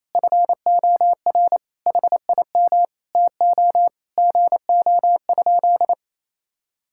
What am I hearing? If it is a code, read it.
Morse code, heard as a keyed tone, about 27 wpm, 700 Hz. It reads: FOR HIM TO GO?
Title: Sentences from Top 1600 Words 28wpm